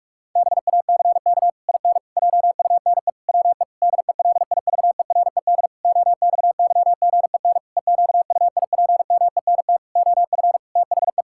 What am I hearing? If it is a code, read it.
Morse code, heard as a keyed tone, about 46 wpm, 700 Hz. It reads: BACK IN JUNE WE DELIVERED OXYGEN EQUIPMENT OF THE
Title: Pangram Challenge v2 - 45wpm